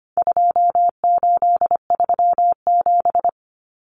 Morse code, heard as a keyed tone, about 25 wpm, 700 Hz. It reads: 2837
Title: Common Cut Numbers - 4-Digit 25wpm